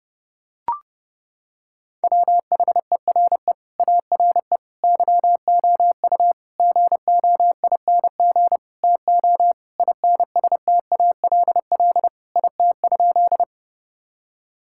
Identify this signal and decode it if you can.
Morse code, heard as a keyed tone, about 30 wpm, 700 Hz. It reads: WHERE ARE YOU GOING TO INSTALL IT?
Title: Sentences from Top 1600 Words 30wpm